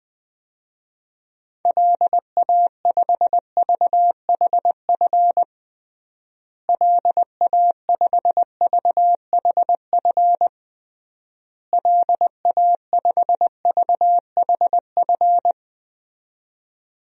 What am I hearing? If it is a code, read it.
Morse code, heard as a keyed tone, about 20 wpm, 700 Hz. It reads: LA5VHF LA5VHF LA5VHF
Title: DXCC-100 Call Signs - 6 Characters - Repeated 3x - 20wpm